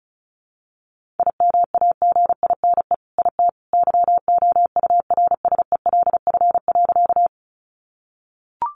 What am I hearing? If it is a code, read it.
Morse code, heard as a keyed tone, about 34 wpm, 700 Hz. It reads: IMAGINE IT YOURSELF.
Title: Sentences from Top 1400 Words 35wpm